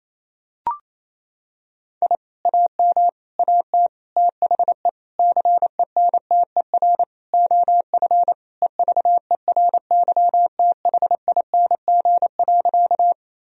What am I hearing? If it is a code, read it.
Morse code, heard as a keyed tone, about 28 wpm, 700 Hz. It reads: I AM AT THE CENTER OF EVERYTHING.
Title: Sentences from Top 1000 Words - Review Part 2 - 28wpm